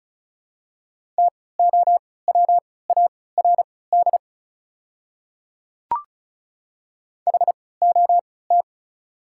Morse code, heard as a keyed tone, about 36 wpm, spaced as for 20 wpm, 690 Hz. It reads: TOWARD HOT
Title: 3x Character Spacing - Top 500 Words - 35wpm